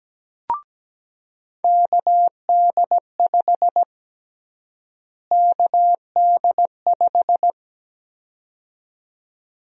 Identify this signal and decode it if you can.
Morse code, heard as a keyed tone, about 17 wpm, 700 Hz. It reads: KD5 KD5
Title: USA Call Sign Prefixes - Repeated 2x - 17wpm